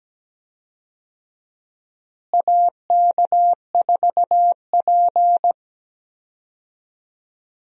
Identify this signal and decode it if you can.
Morse code, heard as a keyed tone, about 17 wpm, 700 Hz. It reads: AK4P
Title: USA Call Signs - 4 Characters - Repeated 2x - 17wpm